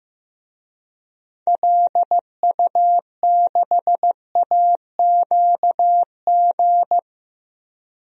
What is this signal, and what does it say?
Morse code, heard as a keyed tone, about 15 wpm, 700 Hz. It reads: LU6AQG
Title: DXCC-100 Call Signs - 6 Characters - 15wpm